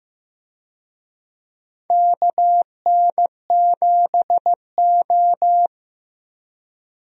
Morse code, heard as a keyed tone, about 15 wpm, 700 Hz. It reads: KN7O
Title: USA Call Signs - 4 Characters - Repeated 2x - 15wpm